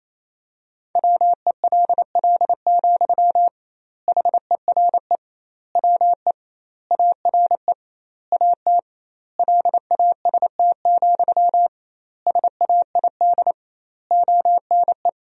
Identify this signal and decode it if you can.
Morse code, heard as a keyed tone, about 28 wpm, 700 Hz. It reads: WELL, HERE WE ARE AT LAST, SAID ONE
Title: 2x Word-Spacing - Sentences from Top 300 Words - 28wpm